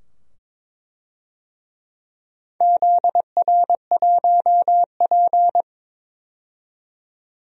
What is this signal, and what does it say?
Morse code, heard as a keyed tone, about 22 wpm, 700 Hz. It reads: ZR1P
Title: DXCC-100 Call Signs - 4 Characters - 22wpm